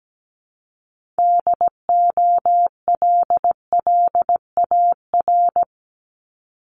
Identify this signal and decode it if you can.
Morse code, heard as a keyed tone, about 17 wpm, 700 Hz. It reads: DOLLAR
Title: Multi-letter Phonemes Repeated 3x 17wpm